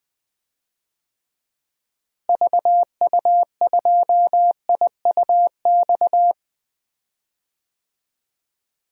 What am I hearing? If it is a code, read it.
Morse code, heard as a keyed tone, about 20 wpm, 700 Hz. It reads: VU2IUX
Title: DXCC-100 Call Signs - 6 Characters - Repeated 2x - 20wpm